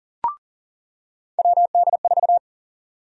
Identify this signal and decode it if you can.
Morse code, heard as a keyed tone, about 38 wpm, 700 Hz. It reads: WD4
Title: USA Call Sign Prefixes - 40wpm